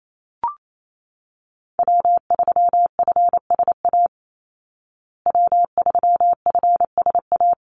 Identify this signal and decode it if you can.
Morse code, heard as a keyed tone, about 28 wpm, 700 Hz. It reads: W3FSA W3FSA
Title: Call Signs Repeated 3x - 28wpm